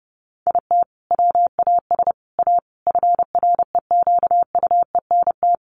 Morse code, heard as a keyed tone, about 28 wpm, 700 Hz. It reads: IT WAS A FREQUENT
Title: Sentences from Top 1400 Words 30wpm